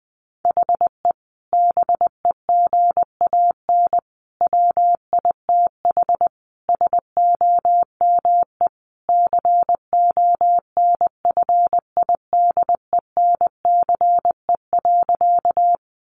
Morse code, heard as a keyed tone, about 19 wpm, 700 Hz. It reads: HE BEGAN WITH SOME CONFIDENCE.
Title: Sentences from Top 1500 Words 20wpm